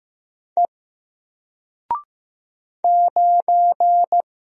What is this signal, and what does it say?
Morse code, heard as a keyed tone, about 15 wpm, 700 Hz. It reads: E 9
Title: Single Letter-Number 15wpm